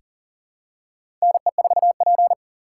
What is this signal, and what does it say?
Morse code, heard as a keyed tone, about 40 wpm, 690 Hz. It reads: NE4P
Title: USA Call Signs - 3x Character Spacing - 40wpm